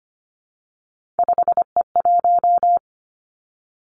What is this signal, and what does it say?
Morse code, heard as a keyed tone, about 25 wpm, 700 Hz. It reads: 5E1
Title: Three Challenging Letter-Number Combinations 25wpm